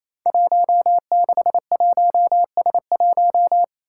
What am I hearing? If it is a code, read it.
Morse code, heard as a keyed tone, about 28 wpm, 700 Hz. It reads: 161S1
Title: Five Challenging Letter-Number Combinations 28wpm